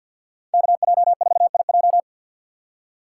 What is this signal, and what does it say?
Morse code, heard as a keyed tone, about 50 wpm, 700 Hz. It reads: KJ4IJ
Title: USA Call Signs - 5 Characters - 4x Character Spacing - 50wpm